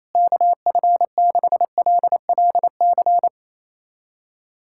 Morse code, heard as a keyed tone, about 28 wpm, 700 Hz. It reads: KF6LLC
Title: USA Callsign DE Callsign - 6 Characters - 28wpm